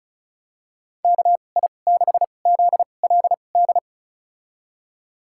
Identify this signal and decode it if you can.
Morse code, heard as a keyed tone, about 36 wpm, spaced as for 25 wpm, 700 Hz. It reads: KI6ZLD
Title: USA Call Signs - 6 Characters - 2x Character Spacing - 35wpm